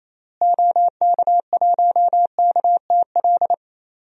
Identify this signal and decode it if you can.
Morse code, heard as a keyed tone, about 27 wpm, 700 Hz. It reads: OK1KTL
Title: DXCC-100 Call Signs - 6 Characters - Repeated 2x - 28wpm